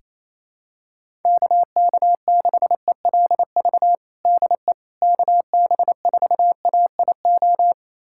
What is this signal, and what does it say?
Morse code, heard as a keyed tone, about 28 wpm, 700 Hz. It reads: KK6ELV DE KB4AIO
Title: USA Callsign DE Callsign - 6 Characters - 28wpm